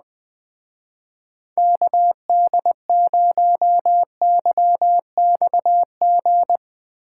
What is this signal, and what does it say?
Morse code, heard as a keyed tone, about 20 wpm, 700 Hz. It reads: KD0YXG
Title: Speed Racing - USA 2x3 Call Signs 17wpm